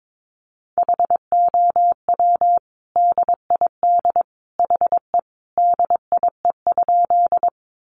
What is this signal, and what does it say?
Morse code, heard as a keyed tone, about 22 wpm, 700 Hz. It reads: HOW DID HE DIE?
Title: Sentences from Top 1300 Words 22wpm